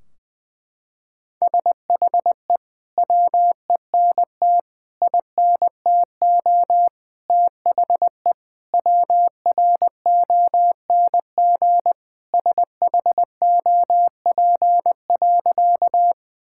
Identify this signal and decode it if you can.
Morse code, heard as a keyed tone, about 21 wpm, 700 Hz. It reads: SHE WENT INTO THE WRONG SHOP.
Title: Sentences from Top 1600 Words 20wpm